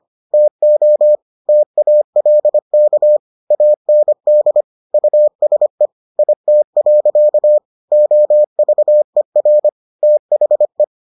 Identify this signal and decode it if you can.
Morse code, heard as a keyed tone, about 25 wpm, 600 Hz. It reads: TO TALK AND USE IT. OVER THE